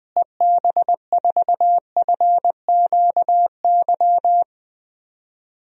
Morse code, heard as a keyed tone, about 20 wpm, 700 Hz. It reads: EB4FQY
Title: DXCC-100 Call Signs - 6 Characters - Repeated 3x - 20wpm